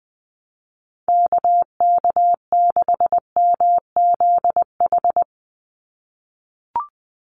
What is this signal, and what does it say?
Morse code, heard as a keyed tone, about 20 wpm, 700 Hz. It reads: KK6MZH
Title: USA Call Signs 20wpm